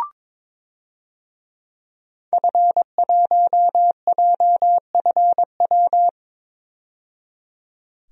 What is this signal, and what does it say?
Morse code, heard as a keyed tone, about 21 wpm, 700 Hz. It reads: F1JFW
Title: DXCC-100 Call Signs - 5 Characters - 22wpm